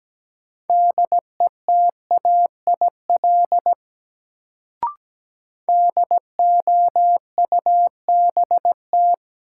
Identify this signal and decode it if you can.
Morse code, heard as a keyed tone, about 17 wpm, 700 Hz. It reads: DETAIL DOUBT
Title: Sentences from Top 1500 Words - New Word Drill - 17wpm